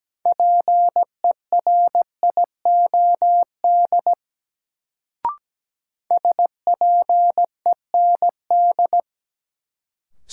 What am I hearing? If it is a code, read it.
Morse code, heard as a keyed tone, about 17 wpm, 700 Hz. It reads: PERIOD SPEND
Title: Sentences from Top 800 Words - New Word Drill - 17wpm